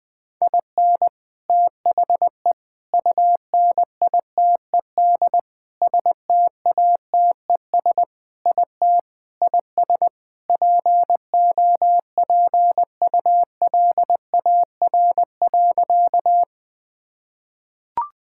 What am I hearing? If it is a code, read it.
Morse code, heard as a keyed tone, about 20 wpm, 700 Hz. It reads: IN THE UNITED STATES IT IS POPULAR.
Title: Sentences with 3-Word Phrases - 20wpm